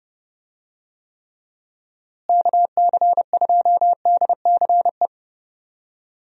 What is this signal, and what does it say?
Morse code, heard as a keyed tone, about 30 wpm, 700 Hz. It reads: KC2DCE
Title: USA Call Signs - 6 Characters - Repeated 3x - 30wpm